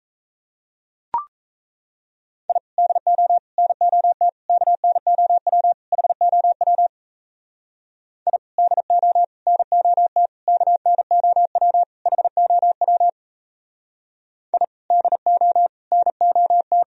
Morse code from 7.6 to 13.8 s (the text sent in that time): I DO NOT KNOW HOW